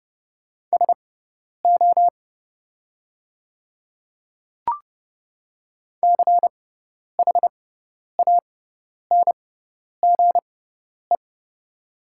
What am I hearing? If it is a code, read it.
Morse code, heard as a keyed tone, about 30 wpm, spaced as for 10 wpm, 700 Hz. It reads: SO CHANGE
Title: 6x Character Spacing - Top 200 Words - 30wpm